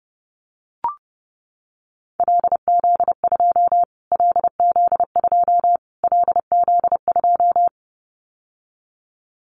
Morse code, heard as a keyed tone, about 30 wpm, 700 Hz. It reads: LZ2 LZ2 LZ2
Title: Callsign Prefixes for Top 25 DXCC - Repeated 3x - 30wpm